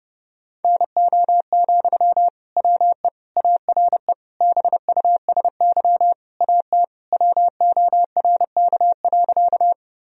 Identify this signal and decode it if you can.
Morse code, heard as a keyed tone, about 30 wpm, 700 Hz. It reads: NO, WE ARE BUSY AT WORK.
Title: Sentences from Top 1000 Words - Review Part 3 - 30wpm